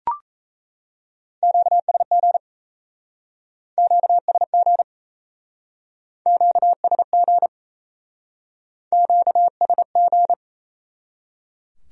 Morse code, heard as a keyed tone, about 35 wpm, 700 Hz. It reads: QSG QSG QSG QSG
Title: Speed-Racing Q-Codes - 28wpm